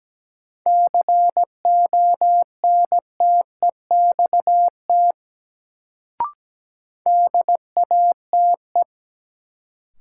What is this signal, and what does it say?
Morse code, heard as a keyed tone, about 17 wpm, 700 Hz. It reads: CONTEXT DATE
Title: Sentences from Top 1000 Words - Review Part 7 - New Word Drill - 17wpm